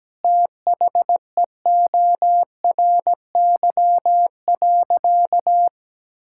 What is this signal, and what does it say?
Morse code, heard as a keyed tone, about 17 wpm, 700 Hz. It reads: THEORY.
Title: Sentences from Top 1000 Words - Review Part 5 - 17wpm